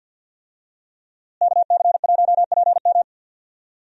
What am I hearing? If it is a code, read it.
Morse code, heard as a keyed tone, about 52 wpm, 700 Hz. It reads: KC1PM
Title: USA Call Signs - 5 Characters - Repeated 2x - 50wpm